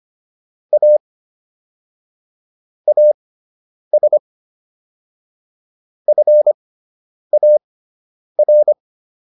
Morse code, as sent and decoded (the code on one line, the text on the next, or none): .- / .- ... / ..-. .- .-.
A AS FAR